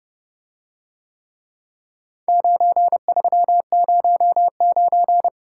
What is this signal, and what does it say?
Morse code, heard as a keyed tone, about 30 wpm, 700 Hz. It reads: 9309